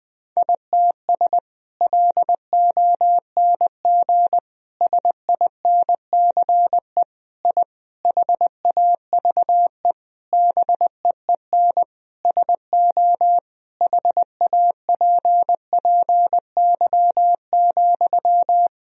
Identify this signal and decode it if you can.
Morse code, heard as a keyed tone, about 20 wpm, 700 Hz. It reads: ITS LONG SINCE I HAVE BEEN SO HAPPY,
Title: Sentences from Top 1000 Words - Review Part 1 - 20wpm